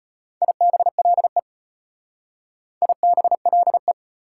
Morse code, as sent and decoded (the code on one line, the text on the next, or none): .. -... .-.. / .. -... .-.. .
IBLE IBLE